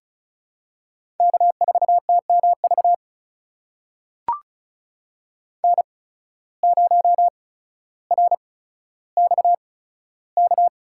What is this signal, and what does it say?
Morse code, heard as a keyed tone, about 34 wpm, spaced as for 15 wpm, 700 Hz. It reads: K4TMV N0RXK
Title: USA Call Signs - 5 Characters - 8x Character Spacing - 35wpm